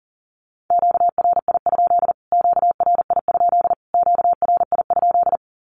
Morse code, heard as a keyed tone, about 40 wpm, 700 Hz. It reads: QRI? QRI? QRI?
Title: Q-Codes Repeated 3x 40wpm